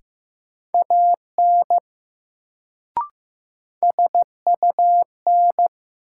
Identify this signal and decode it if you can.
Morse code, heard as a keyed tone, about 15 wpm, 700 Hz. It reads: AN SUN
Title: Sentences from Top 300 Words - New Word Drill - 15wpm